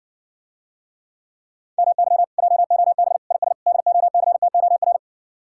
Morse code, heard as a keyed tone, about 60 wpm, 700 Hz. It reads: MY JOB IS DOCTOR